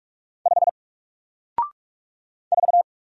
Morse code, heard as a keyed tone, about 45 wpm, 700 Hz. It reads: 5 4
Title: Single Letter-Number Alt 45wpm